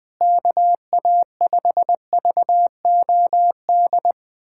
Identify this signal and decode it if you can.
Morse code, heard as a keyed tone, about 20 wpm, 700 Hz. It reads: KA5VOD
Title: USA Callsign DE Callsign 20wpm